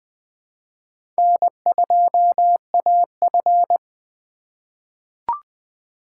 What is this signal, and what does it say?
Morse code, heard as a keyed tone, about 20 wpm, 700 Hz. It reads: N2AF
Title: 2023 SOTA Chaser Call Signs - 20wpm